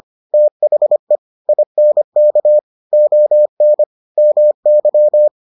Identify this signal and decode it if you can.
Morse code, heard as a keyed tone, about 25 wpm, 600 Hz. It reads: THE INK ON MY